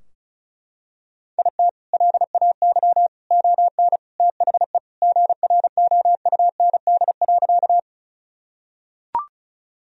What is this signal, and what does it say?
Morse code, heard as a keyed tone, about 36 wpm, 700 Hz. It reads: IT LAY ON THE GROUND.
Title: Sentences from Top 1000 Words - Review Part 2 - 35wpm